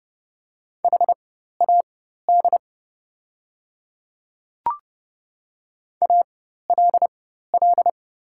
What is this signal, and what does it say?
Morse code, heard as a keyed tone, about 29 wpm, spaced as for 14 wpm, 700 Hz. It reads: HAD ALL
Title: 4x Character Spacing - Top 100 Words - 30wpm